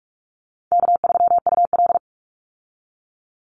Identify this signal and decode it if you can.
Morse code, heard as a keyed tone, about 44 wpm, 700 Hz. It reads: K3UL
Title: Call Signs Repeated 2x - 45wpm